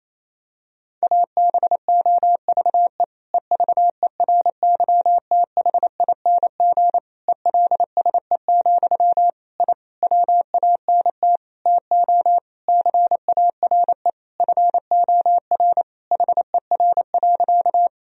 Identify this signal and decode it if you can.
Morse code, heard as a keyed tone, about 27 wpm, 700 Hz. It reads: ABOVE EVERYTHING ELSE, I WANT TO CARE FOR HER.
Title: Sentences from Top 1000 Words - Review Part 2 - 28wpm